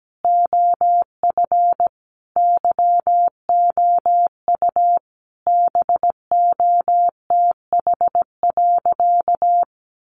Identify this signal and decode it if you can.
Morse code, heard as a keyed tone, about 17 wpm, 700 Hz. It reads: OF YOU BOTH.
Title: Sentences from Top 300 Words 17wpm